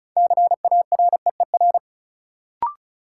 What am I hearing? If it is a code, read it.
Morse code, heard as a keyed tone, about 34 wpm, 700 Hz. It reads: CAREER
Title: Sentences from Top 1000 Words - New Word Drill - 35wpm